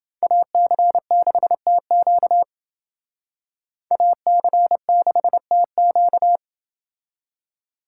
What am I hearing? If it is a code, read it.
Morse code, heard as a keyed tone, about 29 wpm, 700 Hz. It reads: AC6TQ AC6TQ